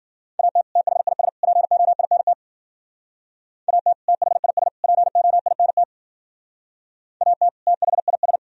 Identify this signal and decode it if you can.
Morse code, heard as a keyed tone, about 51 wpm, 700 Hz. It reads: AT THIS POINT AT THIS POINT AT THIS